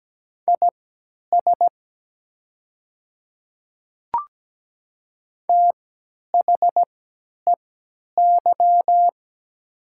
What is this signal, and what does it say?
Morse code, heard as a keyed tone, about 17 wpm, spaced as for 10 wpm, 700 Hz. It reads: IS THEY
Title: 3x Character Spacing - Top 500 Words - 17wpm